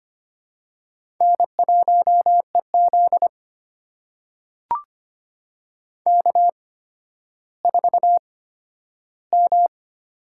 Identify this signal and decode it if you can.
Morse code, heard as a keyed tone, about 25 wpm, spaced as for 12 wpm, 700 Hz. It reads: N1EZ K4M
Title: USA Call Signs - 4 Characters - 8x Character Spacing - 25wpm